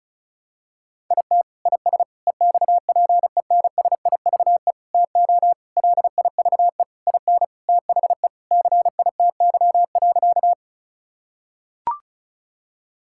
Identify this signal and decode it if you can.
Morse code, heard as a keyed tone, about 36 wpm, 700 Hz. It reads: IT IS EXPENSIVE TO LIVE IN THE CITY.
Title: Sentences from Top 1000 Words - Review Part 6 - 35wpm